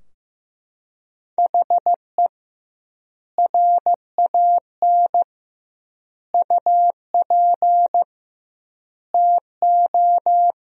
Morse code, heard as a keyed tone, about 15 wpm, 700 Hz. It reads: HE RAN UP TO
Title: 2x Word-Spacing - Sentences from Top 500 Words - 15wpm